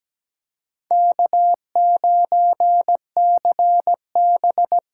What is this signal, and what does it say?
Morse code, heard as a keyed tone, about 20 wpm, 700 Hz. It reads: K9CB
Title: Speed Racing - USA Call Signs 17wpm